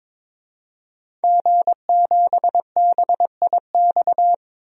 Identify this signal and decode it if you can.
Morse code, heard as a keyed tone, about 22 wpm, 700 Hz. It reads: G7BIX
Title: DXCC-100 Call Signs - 5 Characters - 22wpm